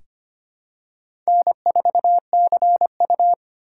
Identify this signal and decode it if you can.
Morse code, heard as a keyed tone, about 25 wpm, 700 Hz. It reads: N4CU
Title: USA Call Signs - 4 Characters - 3x Character Spacing - 25wpm